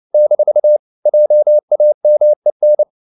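Morse code, heard as a keyed tone, about 29 wpm, 600 Hz. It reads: - JAMEN
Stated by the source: Fast (QRQ) Morse at 29 wpm, from CW med Gnister.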